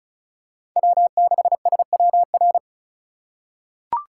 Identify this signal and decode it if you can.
Morse code, heard as a keyed tone, about 35 wpm, 700 Hz. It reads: W6SWR
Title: USA Call Signs - 5 Characters - 2x Character Spacing - 35wpm